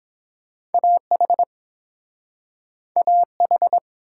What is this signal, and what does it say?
Morse code, heard as a keyed tone, about 32 wpm, 700 Hz. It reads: A H A H